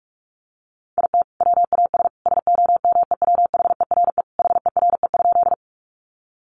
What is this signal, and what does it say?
Morse code, heard as a keyed tone, about 44 wpm, 700 Hz. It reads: IT WAS SOMEWHERE HERE?
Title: Sentences from Top 1400 Words 45wpm